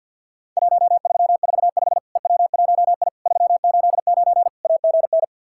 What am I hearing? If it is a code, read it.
Morse code, heard as a keyed tone, about 50 wpm, 700 Hz. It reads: 1345 EW1I 389 AGN
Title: Worked All Europe DX Contest - QTC Exchange with Repeats - 50wpm